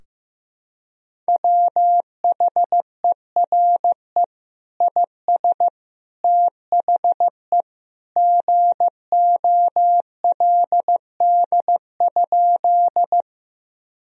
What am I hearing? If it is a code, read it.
Morse code, heard as a keyed tone, about 15 wpm, 700 Hz. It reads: WHERE IS THE GOLD?
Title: Sentences from Top 1000 Words - Review Part 3 - 15wpm